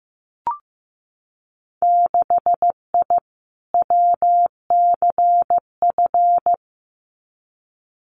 Morse code, heard as a keyed tone, about 15 wpm, 700 Hz. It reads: 6I WCF
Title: Winter Field Day Exchanges Encore 15wpm